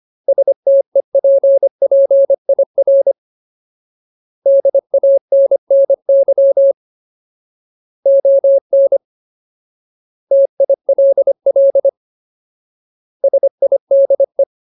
Morse code, heard as a keyed tone, about 25 wpm, 550 Hz. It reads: STEPPIR DANNY ON TILL SIDE
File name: Ultimate Word List / 25WPM Ultimate